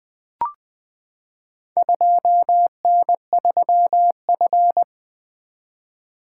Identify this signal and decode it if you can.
Morse code, heard as a keyed tone, about 20 wpm, 700 Hz. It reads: 2N3F
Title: Four Letter-Number Combinations - 20wpm